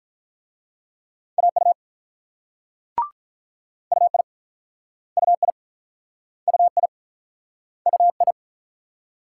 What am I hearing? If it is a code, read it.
Morse code, heard as a keyed tone, about 45 wpm, 700 Hz. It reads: AU UI UI UI UI